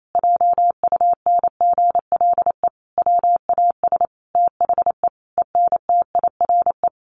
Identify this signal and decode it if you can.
Morse code, heard as a keyed tone, about 28 wpm, 700 Hz. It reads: JUNGLE WAS THE ENTIRE
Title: A Sound Of Thunder 28wpm